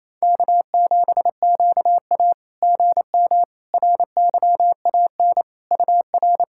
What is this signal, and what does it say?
Morse code, heard as a keyed tone, about 28 wpm, 690 Hz. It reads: K7QA GM RYAN UR
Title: SOTA QSO Practice 28wpm